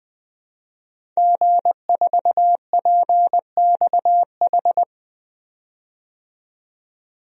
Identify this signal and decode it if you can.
Morse code, heard as a keyed tone, about 20 wpm, 700 Hz. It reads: G4PXH
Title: DXCC-100 Call Signs - 5 Characters - 20wpm